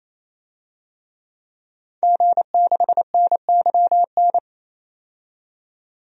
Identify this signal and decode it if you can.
Morse code, heard as a keyed tone, about 28 wpm, 700 Hz. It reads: G6NYN
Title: DXCC-100 Call Signs - 5 Characters - 28wpm